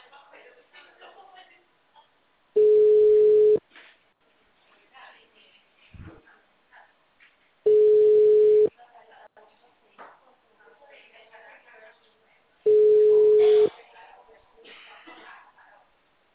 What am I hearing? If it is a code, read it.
Ringback tone (Europe, 425 Hz), with speech.